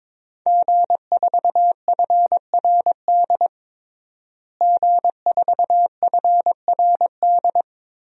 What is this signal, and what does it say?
Morse code, heard as a keyed tone, about 22 wpm, 700 Hz. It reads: G4FRD G4FRD
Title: DXCC-100 Call Signs - 5 Characters - Repeated 2x - 22wpm